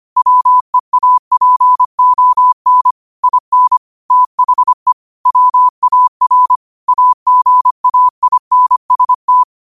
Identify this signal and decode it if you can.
Morse code, heard as a keyed tone, about 25 wpm, 1000 Hz. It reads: WEAPON IN THE WAR AGAINST
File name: qotdfiles / 25wpm